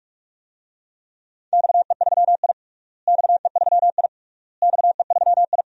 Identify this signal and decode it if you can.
Morse code, heard as a keyed tone, about 47 wpm, 690 Hz. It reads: XE3I XE3I XE3I
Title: DXCC-100 Call Signs - 4 Characters - Repeated 3x - 45wpm